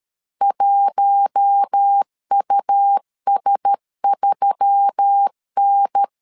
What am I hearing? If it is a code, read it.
Morse code, heard as a keyed tone, about 13 wpm, 800 Hz. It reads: 1US3N
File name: LEZIONI / 5 LBVC / 5 L 12wpm 1913